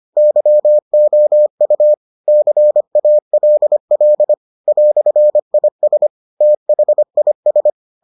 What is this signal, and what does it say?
Morse code, heard as a keyed tone, about 25 wpm, 600 Hz. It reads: YOU CALL "IS THIS